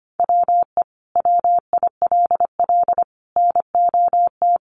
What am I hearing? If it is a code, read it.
Morse code, heard as a keyed tone, about 25 wpm, 700 Hz. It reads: WE WILL NOT
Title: Sentences from Top 1300 Words 25wpm